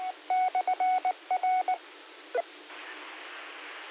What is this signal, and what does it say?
Morse code, heard as a keyed tone, about 19 wpm, 740 Hz. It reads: /R
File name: qsorec SimplexLogic 17-29-45